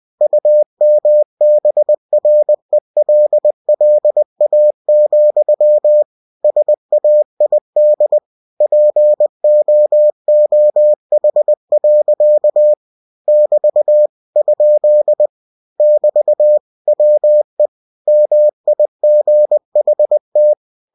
Morse code, heard as a keyed tone, about 20 wpm, 600 Hz. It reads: UMBRELLA, SAID POOH. <BT> ? <BT> WE MIGHT